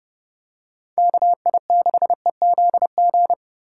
Morse code, heard as a keyed tone, about 30 wpm, 700 Hz. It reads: KI6EZG
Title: USA Call Signs - 6 Characters - 4x Character Spacing - 30wpm